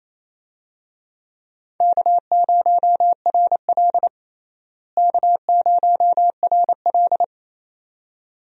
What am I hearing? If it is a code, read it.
Morse code, heard as a keyed tone, about 28 wpm, 700 Hz. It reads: K0RL K0RL
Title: USA Call Signs - 4 Characters - Repeated 2x - 28wpm